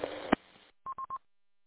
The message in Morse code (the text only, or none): none